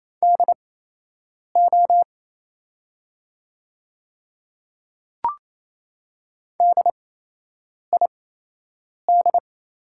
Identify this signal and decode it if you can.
Morse code, heard as a keyed tone, about 28 wpm, spaced as for 8 wpm, 700 Hz. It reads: DO DID
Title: 8x Character Spacing - Top 300 Words - 28wpm